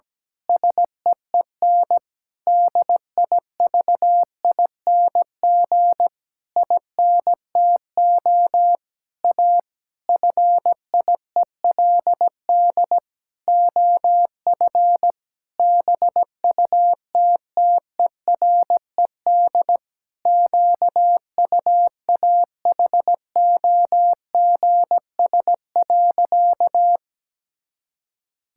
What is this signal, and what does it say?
Morse code, heard as a keyed tone, about 17 wpm, 700 Hz. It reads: SEEN DIVING INTO A FIELD OF BUTTERED QUAHOGS.
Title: Pangram Challenge v2 - 17wpm